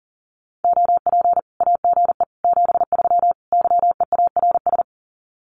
Morse code, heard as a keyed tone, about 40 wpm, 700 Hz. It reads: OP AGE 73 YEARS
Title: Common QSO Phrases 40wpm